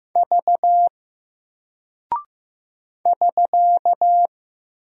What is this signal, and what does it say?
Morse code, heard as a keyed tone, about 15 wpm, 700 Hz. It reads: V <SK>
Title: Single Character (Letter:Number:Punctuation:Prosign) Alt 15wpm